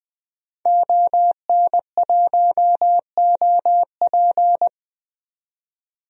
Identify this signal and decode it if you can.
Morse code, heard as a keyed tone, about 20 wpm, 700 Hz. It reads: ON1OP
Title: DXCC-100 Call Signs - 5 Characters - 20wpm